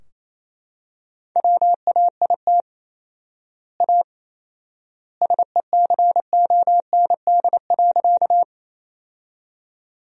Morse code, heard as a keyed tone, about 28 wpm, 700 Hz. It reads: WAIT A SECOND.
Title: 4x Word-Spacing - Sentences from Top 500 Words - 28wpm